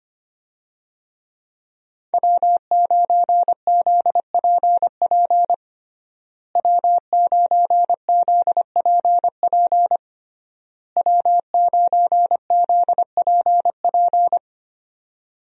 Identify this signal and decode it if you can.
Morse code, heard as a keyed tone, about 25 wpm, 700 Hz. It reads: W9ZPP W9ZPP W9ZPP
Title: USA Call Signs - 5 Characters - Repeated 3x - 25wpm